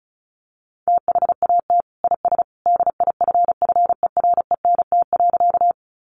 Morse code, heard as a keyed tone, about 34 wpm, 700 Hz. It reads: THAT IS DIFFERENT.
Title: Sentences from Top 600 Words 35wpm